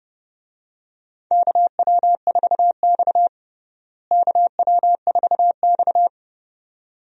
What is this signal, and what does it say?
Morse code, heard as a keyed tone, about 29 wpm, 700 Hz. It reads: KW4X KW4X
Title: USA Call Signs - 4 Characters - Repeated 2x - 30wpm